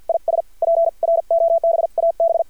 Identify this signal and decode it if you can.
Morse code, heard as a keyed tone, about 50 wpm, spaced as for 43 wpm, 670 Hz. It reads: IS PROBAB
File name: archived / CRSnet